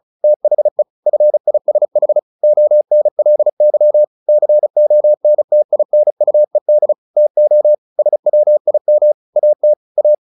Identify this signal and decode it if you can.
Morse code, heard as a keyed tone, about 34 wpm, 600 Hz. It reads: THE FISH ONLY CONTINUED TO SWIM AT A
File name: the old man and the sea / the old man and the sea 1 35wpm